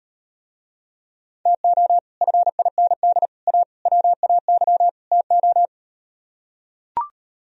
Morse code, heard as a keyed tone, about 38 wpm, 700 Hz. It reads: TO FIND A WAY TO